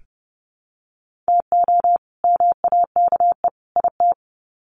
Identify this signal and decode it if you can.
Morse code, heard as a keyed tone, about 30 wpm, 700 Hz. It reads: TO MAKE IT
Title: Sets of 3 Words 30wpm